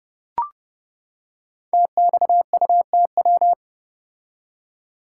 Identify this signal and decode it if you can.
Morse code, heard as a keyed tone, about 30 wpm, 700 Hz. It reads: TXUTW
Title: Code Groups - 5 Letters - 30wpm